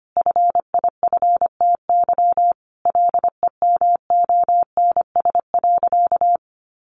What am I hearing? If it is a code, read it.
Morse code, heard as a keyed tone, about 25 wpm, 700 Hz. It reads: FIFTY LEMONS.
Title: Additional Easy Phrases 25wpm